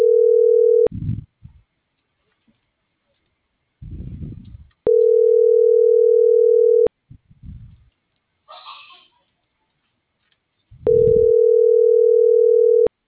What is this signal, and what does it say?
Ringback tone (North America), 440 + 480 Hz.